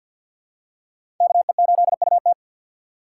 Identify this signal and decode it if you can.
Morse code, heard as a keyed tone, about 50 wpm, 700 Hz. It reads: KE8UT
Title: USA Call Signs - 5 Characters - 6x Character Spacing - 50wpm